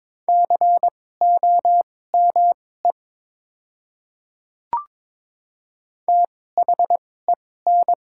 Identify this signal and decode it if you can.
Morse code, heard as a keyed tone, about 22 wpm, spaced as for 16 wpm, 700 Hz. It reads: COME THEN
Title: 2x Character Spacing - Top 100 Words - 22wpm